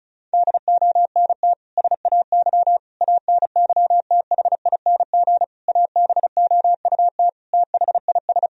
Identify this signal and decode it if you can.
Morse code, heard as a keyed tone, about 34 wpm, 700 Hz. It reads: DONT SAY ANYTHING ABOUT THIS
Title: Sentences from Top 1000 Words - Review Part 1 - 35wpm